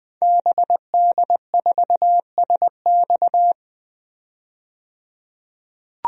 Morse code, heard as a keyed tone, about 20 wpm, 700 Hz. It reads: BD4SX
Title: DXCC-100 Call Signs - 5 Characters - 20wpm